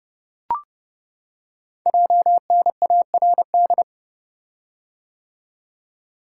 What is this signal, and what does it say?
Morse code, heard as a keyed tone, about 30 wpm, 700 Hz. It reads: JNARD
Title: Code Groups - 5 Letters - 30wpm